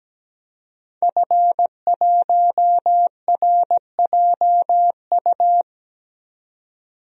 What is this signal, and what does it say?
Morse code, heard as a keyed tone, about 17 wpm, 700 Hz. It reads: F1RJU
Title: DXCC-100 Call Signs - 5 Characters - 17wpm